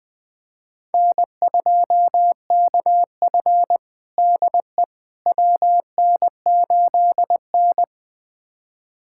Morse code, heard as a keyed tone, about 20 wpm, 700 Hz. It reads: N2KF DE WN8N
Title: USA Callsign DE Callsign - 4 Characters - 20wpm